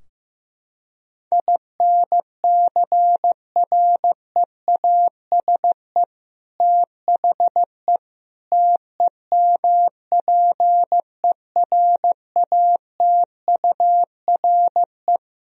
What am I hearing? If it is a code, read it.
Morse code, heard as a keyed tone, about 15 wpm, 700 Hz. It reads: INCREASE THE TEMPERATURE
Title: Sentences from Top 1000 Words - Review Part 4 - 15wpm